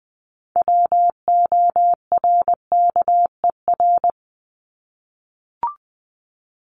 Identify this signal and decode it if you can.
Morse code, heard as a keyed tone, about 20 wpm, 700 Hz. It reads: WORKER
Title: ES:S Suffix - Top 100 Words with Base Words 20wpm